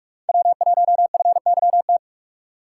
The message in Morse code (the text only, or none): W1FYT